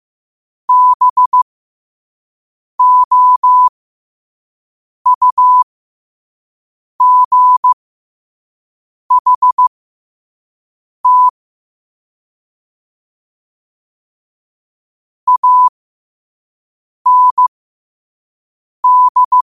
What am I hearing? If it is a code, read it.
Morse code, heard as a keyed tone, about 15 wpm, spaced as for 5 wpm, 1000 Hz. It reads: BOUGHT AND
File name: qotdfiles / 5wpm